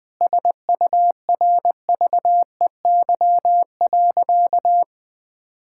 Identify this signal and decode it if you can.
Morse code, heard as a keyed tone, about 20 wpm, 700 Hz. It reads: SURVEY.
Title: Sentences with 7-Word Phrases - Spoken Beginning - 20wpm